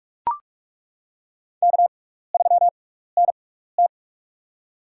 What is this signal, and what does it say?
Morse code, heard as a keyed tone, about 42 wpm, spaced as for 15 wpm, 700 Hz. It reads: K3NT
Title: USA Call Signs - 4 Characters - 6x Character Spacing - 45wpm